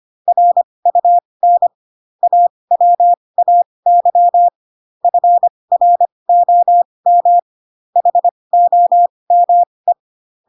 Morse code, heard as a keyed tone, about 25 wpm, spaced as for 20 wpm, 700 Hz. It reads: RUN AWAY FROM HOME